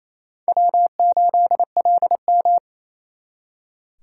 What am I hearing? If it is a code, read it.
Morse code, heard as a keyed tone, about 35 wpm, 700 Hz. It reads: W8LM
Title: Speed Racing - USA 1x2 Call Signs 28wpm